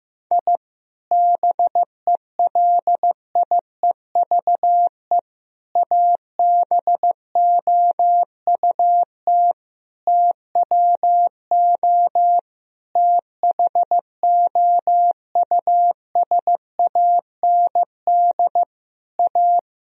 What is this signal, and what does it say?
Morse code, heard as a keyed tone, about 15 wpm, 700 Hz. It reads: I BELIEVE ABOUT TWO THOUSAND A
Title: Sentences from Top 1000 Words - Review Part 3 - 15wpm